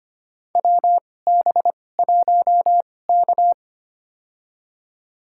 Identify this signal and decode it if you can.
Morse code, heard as a keyed tone, about 25 wpm, spaced as for 18 wpm, 700 Hz. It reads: WB1K
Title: USA Call Signs - 4 Characters - 2x Character Spacing - 25wpm